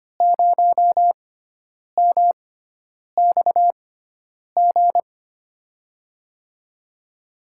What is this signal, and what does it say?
Morse code, heard as a keyed tone, about 25 wpm, spaced as for 9 wpm, 700 Hz. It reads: 0MXG
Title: USA Call Signs - 5 Characters - 6x Character Spacing - 25wpm